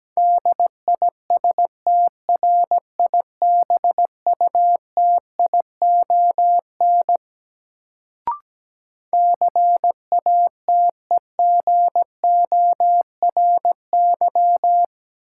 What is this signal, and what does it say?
Morse code, heard as a keyed tone, about 17 wpm, 700 Hz. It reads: DISTRIBUTION CATEGORY
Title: Sentences from Top 1000 Words - Review Part 7 - New Word Drill - 17wpm